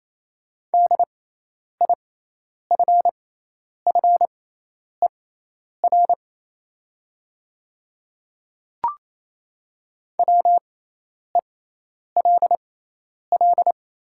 Morse code, heard as a keyed tone, about 28 wpm, spaced as for 10 wpm, 700 Hz. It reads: DIFFER WELL
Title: 6x Character Spacing - Top 200 Words - 28wpm